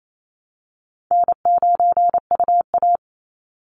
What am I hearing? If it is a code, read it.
Morse code, heard as a keyed tone, about 28 wpm, 700 Hz. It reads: N9UA
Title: Easy Call Signs Repeated 3x - 28wpm